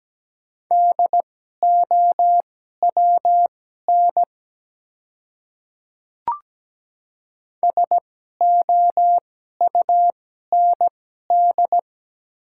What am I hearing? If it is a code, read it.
Morse code, heard as a keyed tone, about 17 wpm, spaced as for 12 wpm, 700 Hz. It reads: DOWN SOUND
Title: 2x Character Spacing - Top 100 Words - 17wpm